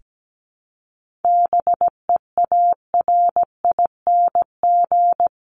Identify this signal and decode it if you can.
Morse code, heard as a keyed tone, about 17 wpm, 700 Hz. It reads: BEARING
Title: ING Suffix - Top 100 Words with Base Words 17wpm